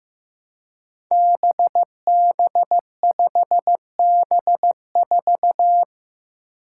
Morse code, heard as a keyed tone, about 15 wpm, 700 Hz. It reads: BB5B4
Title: Five Challenging Letter-Number Combinations 15wpm